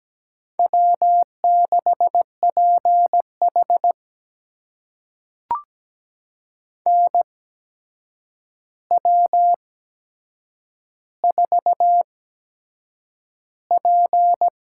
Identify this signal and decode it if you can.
Morse code, heard as a keyed tone, about 17 wpm, spaced as for 7 wpm, 700 Hz. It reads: W6PH NW4P
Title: USA Call Signs - 4 Characters - 8x Character Spacing - 17wpm